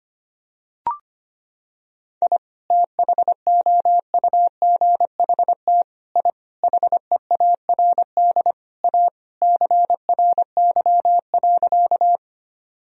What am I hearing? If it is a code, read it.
Morse code, heard as a keyed tone, about 25 wpm, 700 Hz. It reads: I THOUGHT I HEARD A CRY.
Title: Sentences from Top 1000 Words - Review Part 2 - 25wpm